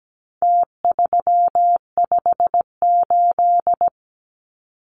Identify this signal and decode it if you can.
Morse code, heard as a keyed tone, about 17 wpm, 700 Hz. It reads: T358
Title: Numbers - Contest Style - 4-Digit 17wpm